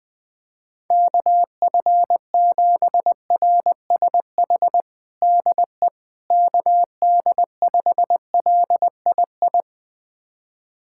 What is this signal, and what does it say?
Morse code, heard as a keyed tone, about 20 wpm, 700 Hz. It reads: KF7RSH DE KD5LII
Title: USA Callsign DE Callsign 20wpm